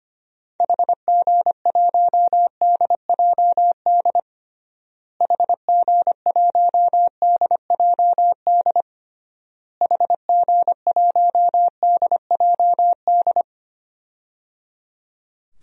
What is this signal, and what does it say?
Morse code, heard as a keyed tone, about 25 wpm, 700 Hz. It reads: HG1DJD HG1DJD HG1DJD
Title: DXCC-100 Call Signs - 6 Characters - Repeated 3x - 25wpm